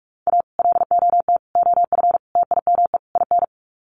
Morse code, heard as a keyed tone, about 44 wpm, 700 Hz. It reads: A LOT OF TIME IN
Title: Sets of 5 Words Encore2 45wpm